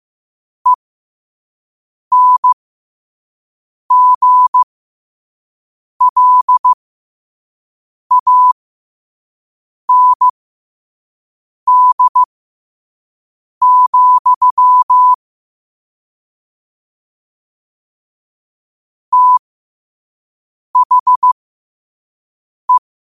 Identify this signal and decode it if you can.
Morse code, heard as a keyed tone, about 15 wpm, spaced as for 5 wpm, 1000 Hz. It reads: ENGLAND, THE
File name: qotdfiles / 5wpm